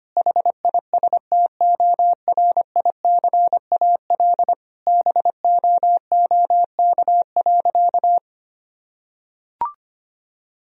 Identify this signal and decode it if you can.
Morse code, heard as a keyed tone, about 25 wpm, 700 Hz. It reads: HISTORICAL BOOK.
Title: Sentences from Top 1000 Words - Review Part 4 - 25wpm